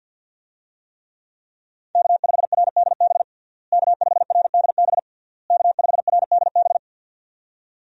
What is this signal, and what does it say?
Morse code, heard as a keyed tone, about 48 wpm, 700 Hz. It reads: K5RDB K5RDB K5RDB
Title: USA Call Signs - 5 Characters - Repeated 3x - 50wpm